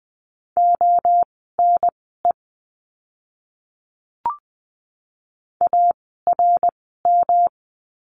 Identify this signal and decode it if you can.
Morse code, heard as a keyed tone, about 20 wpm, spaced as for 15 wpm, 700 Hz. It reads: ONE ARM
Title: 108 Lesson 18f - Letter m - Words 20wpm